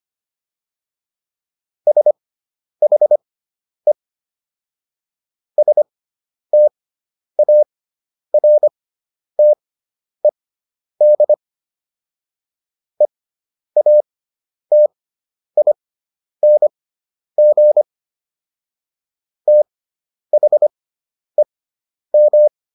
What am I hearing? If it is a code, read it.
Morse code, heard as a keyed tone, about 25 wpm, spaced as for 10 wpm, 600 Hz. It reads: SHE STARTED EATING THEM